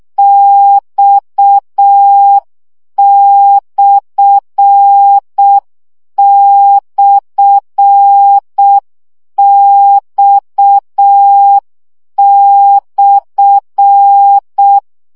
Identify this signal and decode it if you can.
Morse code, heard as a keyed tone, about 6 wpm, 800 Hz. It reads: X//X/
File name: CW gyakorlas x per 5 baud